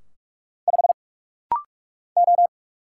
Morse code, heard as a keyed tone, about 44 wpm, 700 Hz. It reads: 5 O
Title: Single Letter-Number - Rapid-Fire 45wpm